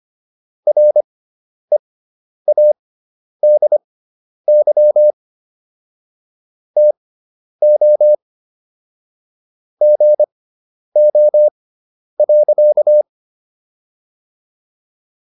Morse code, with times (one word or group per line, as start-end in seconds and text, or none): 0.7-5.1 READY
6.8-8.1 TO
9.8-13.0 GO.